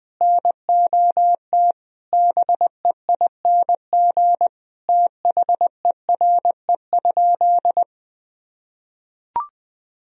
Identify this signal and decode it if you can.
Morse code, heard as a keyed tone, about 21 wpm, 700 Hz. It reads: NOT BEING THERE?
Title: Sentences from Top 1000 Words - Review Part 3 - 20wpm